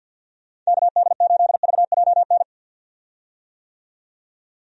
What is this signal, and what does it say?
Morse code, heard as a keyed tone, about 50 wpm, 700 Hz. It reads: KD8VJN
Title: USA Call Signs - 6 Characters - 50wpm